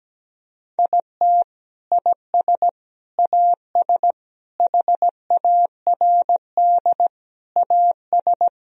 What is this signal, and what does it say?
Morse code, heard as a keyed tone, about 17 wpm, 700 Hz. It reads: IT IS AS HARD AS
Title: Sentences from Top 1000 Words - Review Part 2 - 17wpm